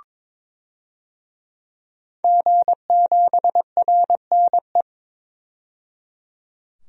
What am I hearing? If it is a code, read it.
Morse code, heard as a keyed tone, about 22 wpm, 700 Hz. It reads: G7RNE
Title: DXCC-100 Call Signs - 5 Characters - 22wpm